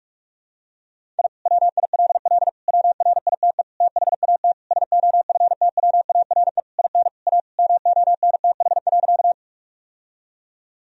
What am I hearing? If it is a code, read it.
Morse code, heard as a keyed tone, about 41 wpm, 700 Hz. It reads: I WILL WRITE THAT SOFTWARE IN A MONTH.
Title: Sentences from Top 1000 Words - Review Part 3 - 45wpm